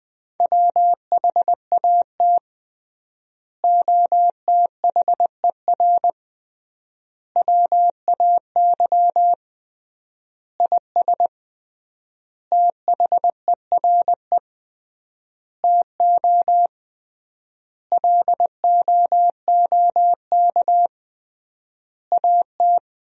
Morse code, heard as a keyed tone, about 21 wpm, 700 Hz. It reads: WHAT OTHER WAY IS THERE TO LOOK AT
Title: 3x Word-Spacing - Sentences from Top 100 Words - 20wpm